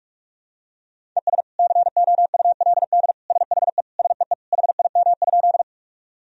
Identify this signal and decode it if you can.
Morse code, heard as a keyed tone, about 41 wpm, 700 Hz. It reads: ES COULD SHE SEE HIM?
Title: Sentences from Top 100 Words and Abbreviations 45wpm